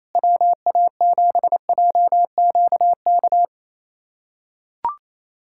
Morse code, heard as a keyed tone, about 28 wpm, 700 Hz. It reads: WA7JQK
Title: USA Callsign DE Callsign 28wpm